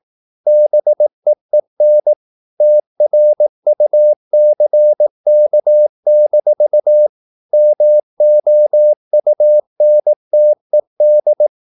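Morse code, heard as a keyed tone, about 18 wpm, 600 Hz. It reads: BEEN TRUCK-MOUNTED
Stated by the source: Morse practice at 18 wpm, English text, from CW med Gnister.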